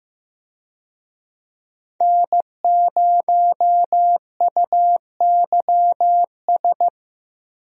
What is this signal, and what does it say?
Morse code, heard as a keyed tone, about 15 wpm, 700 Hz. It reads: N0UYS
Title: USA Call Signs - 5 Characters - Repeated 3x - 15wpm